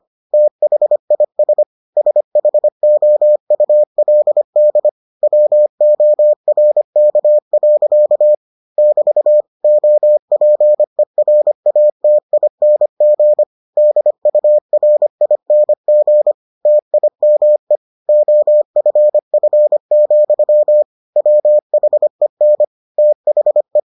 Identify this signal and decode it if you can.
Morse code, heard as a keyed tone, about 25 wpm, 600 Hz. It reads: THIS SHOULD WORK. <BT> OPERATING DURING TIME OFF, WHEN THE